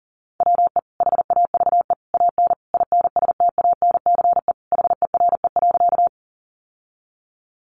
Morse code, heard as a keyed tone, about 40 wpm, 700 Hz. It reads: WE HAVE AN INSTANCE HERE.
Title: Sentences from Top 800 Words 40wpm